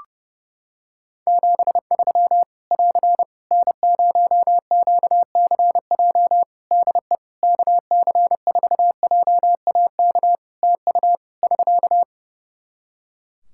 Morse code, heard as a keyed tone, about 30 wpm, 700 Hz. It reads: 73 <AR> N0QCJ DE KC4JAK TU <SK>
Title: K1USN SST QSO Practice 30wpm